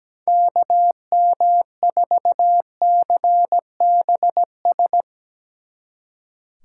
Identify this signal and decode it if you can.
Morse code, heard as a keyed tone, about 18 wpm, 700 Hz. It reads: KM4CBS
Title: Speed Racing - USA 2x3 Call Signs 17wpm